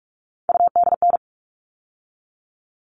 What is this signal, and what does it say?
Morse code, heard as a keyed tone, about 44 wpm, 700 Hz. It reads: UDN
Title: All Cut Numbers - 3-Digit 45wpm